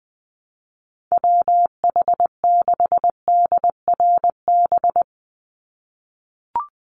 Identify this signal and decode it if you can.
Morse code, heard as a keyed tone, about 20 wpm, 700 Hz. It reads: WH6DRB
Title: USA Call Signs 20wpm